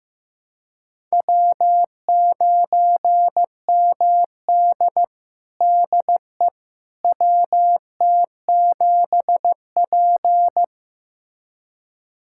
Morse code, heard as a keyed tone, about 15 wpm, 700 Hz. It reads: W9MD DE WT7P
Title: USA Callsign DE Callsign - 4 Characters - 15wpm